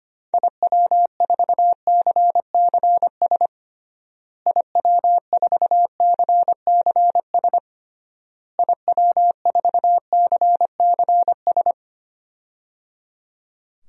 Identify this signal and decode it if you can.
Morse code, heard as a keyed tone, about 25 wpm, 700 Hz. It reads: IW4CCS IW4CCS IW4CCS
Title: DXCC-100 Call Signs - 6 Characters - Repeated 3x - 25wpm